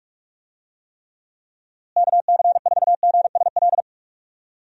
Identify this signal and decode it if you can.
Morse code, heard as a keyed tone, about 45 wpm, 700 Hz. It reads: KC4GSL
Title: USA Call Signs - 6 Characters - Repeated 2x - 45wpm